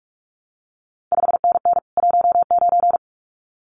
Morse code, heard as a keyed tone, about 46 wpm, 700 Hz. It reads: 5NN 19
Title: CQ WW DX Contest Exchange Encore - 45wpm